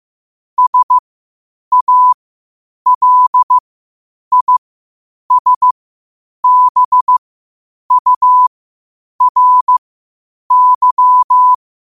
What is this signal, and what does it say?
Morse code, heard as a keyed tone, about 15 wpm, spaced as for 8 wpm, 1000 Hz. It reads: SALISBURY
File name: qotdfiles / 8wpm